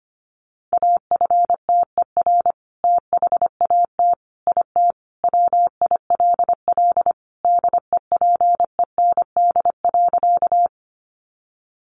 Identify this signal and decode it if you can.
Morse code, heard as a keyed tone, about 25 wpm, 700 Hz. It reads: AFTER THAT IT WILL DEPEND.
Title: Sentences from Top 1300 Words 25wpm